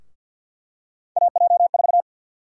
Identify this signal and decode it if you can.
Morse code, heard as a keyed tone, about 50 wpm, 700 Hz. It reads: AJ4
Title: USA Call Sign Prefixes - 50wpm